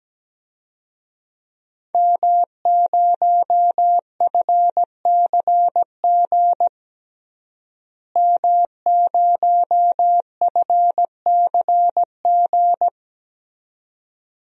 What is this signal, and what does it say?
Morse code, heard as a keyed tone, about 17 wpm, 700 Hz. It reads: M0FCG M0FCG
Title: DXCC-100 Call Signs - 5 Characters - Repeated 2x - 17wpm